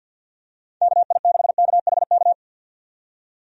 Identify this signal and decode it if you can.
Morse code, heard as a keyed tone, about 51 wpm, 700 Hz. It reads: KI6KHK
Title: USA Call Signs - 6 Characters - 50wpm